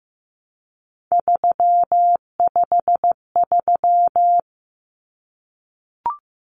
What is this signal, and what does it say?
Morse code, heard as a keyed tone, about 15 wpm, 700 Hz. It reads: 353
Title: Common Cut Numbers - 3-Digit 15wpm